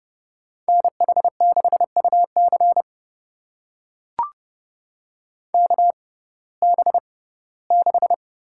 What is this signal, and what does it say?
Morse code, heard as a keyed tone, about 30 wpm, spaced as for 18 wpm, 700 Hz. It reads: NH6UC KB6
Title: USA Call Signs - 5 Characters - 6x Character Spacing - 30wpm